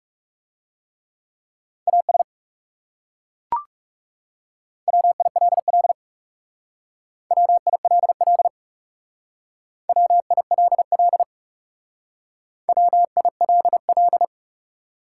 Morse code, heard as a keyed tone, about 39 wpm, 700 Hz. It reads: AS WILL WILL WILL WILL